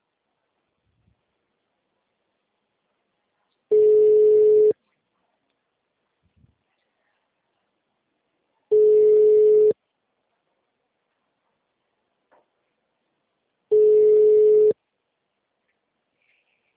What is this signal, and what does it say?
Ringback tone (Europe, 425 Hz).